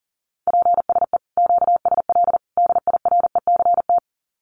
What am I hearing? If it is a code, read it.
Morse code, heard as a keyed tone, about 40 wpm, 700 Hz. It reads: PSE QSL DIRECT
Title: Common QSO Phrases 40wpm